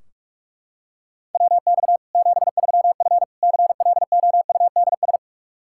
Morse code, heard as a keyed tone, about 41 wpm, 700 Hz. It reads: WX 73F CLOUDS
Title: Practice QSO - Exchange 2 - Info - 45wpm